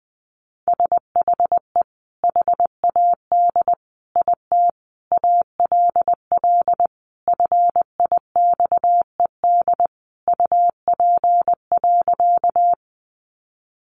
Morse code, heard as a keyed tone, about 20 wpm, 700 Hz. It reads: SHE HAD IT ALL FIXED UP.
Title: Sentences from Top 1100 Words 20wpm